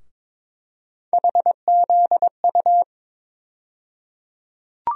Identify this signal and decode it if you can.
Morse code, heard as a keyed tone, about 22 wpm, 700 Hz. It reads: HZU
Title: USA Call Sign Postfixes - 3 Letters 22wpm